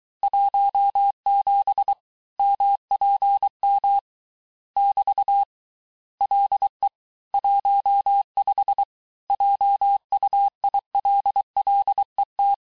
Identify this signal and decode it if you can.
Morse code, heard as a keyed tone, about 23 wpm, 800 Hz. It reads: 17 MPM <BT> LE 15 JUILLET
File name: cours-du-lundi files / cours-1852-texte-clair-vitesse-17MPM-puis-22MPM